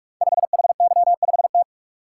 Morse code, heard as a keyed tone, about 44 wpm, 700 Hz. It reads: 5HY5T
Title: Code Groups - 5 Letter:Numbers - 45wpm